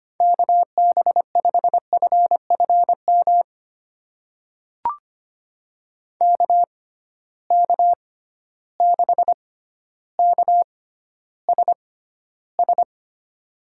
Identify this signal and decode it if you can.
Morse code, heard as a keyed tone, about 25 wpm, spaced as for 13 wpm, 700 Hz. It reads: KB5FFM KK6KSS
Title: USA Call Signs - 6 Characters - 6x Character Spacing - 25wpm